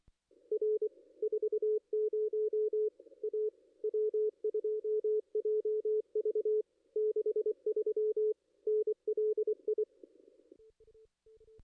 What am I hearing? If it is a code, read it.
Morse code, heard as a keyed tone, about 24 wpm, 430 Hz. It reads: R 40 A W2JV 63 NLI
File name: contests / ss-cw-k5tr-op-n5rz -2300